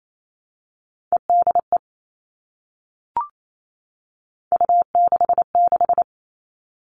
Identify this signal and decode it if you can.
Morse code, heard as a keyed tone, about 28 wpm, 700 Hz. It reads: EDE U66
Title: All Cut Numbers - 3-Digit 28wpm